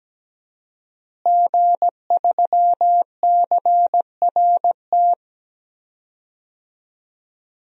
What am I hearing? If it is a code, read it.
Morse code, heard as a keyed tone, about 17 wpm, 700 Hz. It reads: G3CRT
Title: DXCC-100 Call Signs - 5 Characters - Repeated 2x - 17wpm